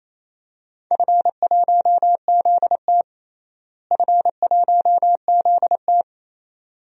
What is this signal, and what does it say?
Morse code, heard as a keyed tone, about 27 wpm, 700 Hz. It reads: F1ZT F1ZT
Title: DXCC-100 Call Signs - 4 Characters - Repeated 2x - 28wpm